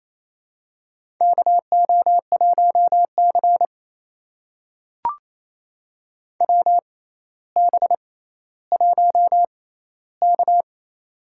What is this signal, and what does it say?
Morse code, heard as a keyed tone, about 28 wpm, spaced as for 14 wpm, 700 Hz. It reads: KO1C WB1K
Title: USA Call Signs - 4 Characters - 6x Character Spacing - 28wpm